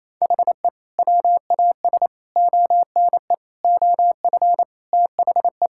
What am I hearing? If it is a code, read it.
Morse code, heard as a keyed tone, about 27 wpm, 700 Hz. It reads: HE WAS ONE OF THE
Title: Sentences from Top 1600 Words 28wpm